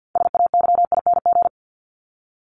Morse code, heard as a keyed tone, about 52 wpm, 700 Hz. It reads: SAYING
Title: ING Suffix - Top 100 Words with Base Words 50wpm